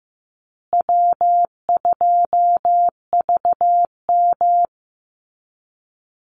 Morse code, heard as a keyed tone, about 15 wpm, 700 Hz. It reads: W2VM
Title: Easy Call Signs Repeated 3x - 15wpm